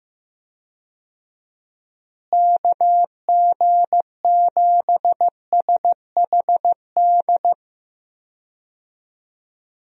Morse code, heard as a keyed tone, about 15 wpm, 700 Hz. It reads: KG7SHD
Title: USA Call Signs - 6 Characters - Repeated 2x - 15wpm